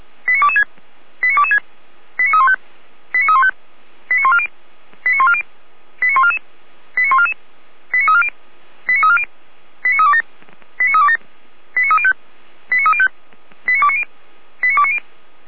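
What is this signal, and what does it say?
Five-tone selective call (ZVEI), heard as quick runs of five tones, tones of about 80 ms. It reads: 89226 89226 89315 89315 89140 89140 89144 89144 89340 89340 89327 89327 89305 89305 89290 89290
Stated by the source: SDR-received ZVEI five-tone selective call, code 89327.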